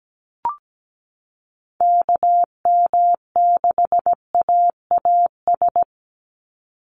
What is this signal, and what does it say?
Morse code, heard as a keyed tone, about 17 wpm, 700 Hz. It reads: KM6AAS
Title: USA Call Signs 17wpm